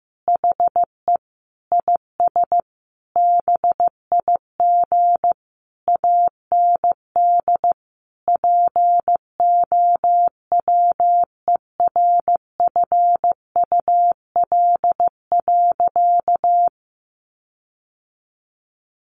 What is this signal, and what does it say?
Morse code, heard as a keyed tone, about 14 wpm, 700 Hz. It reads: HE IS BIG AND POWERFUL.
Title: Sentences from Top 1300 Words 15wpm